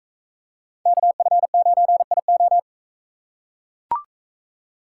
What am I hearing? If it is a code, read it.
Morse code, heard as a keyed tone, about 38 wpm, 700 Hz. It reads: KF9IO